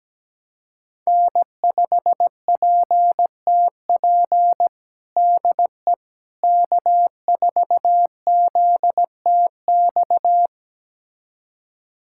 Morse code, heard as a keyed tone, about 17 wpm, 700 Hz. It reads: N5PTP DE K4ZTX
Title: USA Callsign DE Callsign - 5 Characters - 17wpm